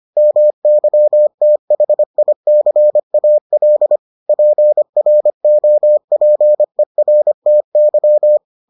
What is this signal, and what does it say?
Morse code, heard as a keyed tone, about 25 wpm, 600 Hz. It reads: MYTHICAL PROPERTY